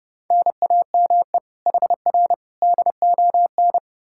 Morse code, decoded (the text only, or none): NAME HR DON